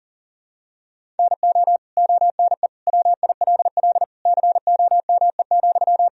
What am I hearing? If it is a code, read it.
Morse code, heard as a keyed tone, about 37 wpm, 700 Hz. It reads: NO ONE WILL COME,
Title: Sentences from Top 100 Words and Abbreviations 40wpm